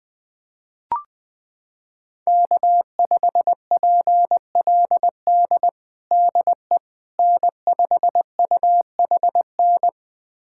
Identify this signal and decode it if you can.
Morse code, heard as a keyed tone, about 21 wpm, 700 Hz. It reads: K5PLD DE N5UHN
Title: USA Callsign DE Callsign - 5 Characters - 20wpm